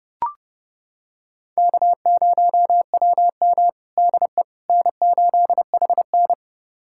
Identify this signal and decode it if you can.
Morse code, heard as a keyed tone, about 29 wpm, 700 Hz. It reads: K0WM DE N8HN
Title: USA Callsign DE Callsign - 4 Characters - 30wpm